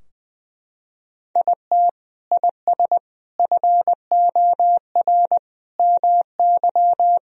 Morse code, heard as a keyed tone, about 20 wpm, 700 Hz. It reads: IT IS FOR MY
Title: Sentences from Top 1000 Words - Review Part 4 - 20wpm